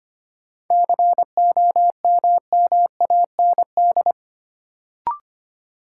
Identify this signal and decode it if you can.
Morse code, heard as a keyed tone, about 25 wpm, 700 Hz. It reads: COMMAND
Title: Sentences from Top 1600 Words - New Word Drill - 25wpm